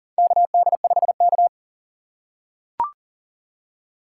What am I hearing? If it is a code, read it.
Morse code, heard as a keyed tone, about 40 wpm, 700 Hz. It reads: KD5K
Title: USA Call Signs - 4 Characters - 2x Character Spacing - 40wpm